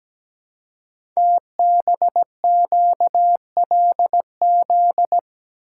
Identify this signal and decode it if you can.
Morse code, heard as a keyed tone, about 17 wpm, 700 Hz. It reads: TBQLZ
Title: Code Groups - 5 Letter:Numbers - 17wpm